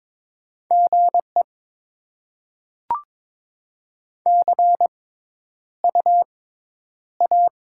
Morse code, heard as a keyed tone, about 22 wpm, 690 Hz. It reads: G E C U A
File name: Top 20 QSO Elements - 6x Character Spacing - 22wpm